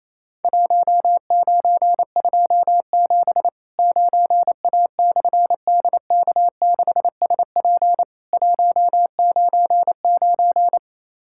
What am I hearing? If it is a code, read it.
Morse code, heard as a keyed tone, about 28 wpm, 700 Hz. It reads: 1927 9A/DK6SP 199
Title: Worked All Europe DX Contest - QTC Exchange - 28wpm